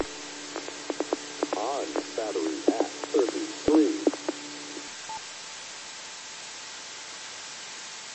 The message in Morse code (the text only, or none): none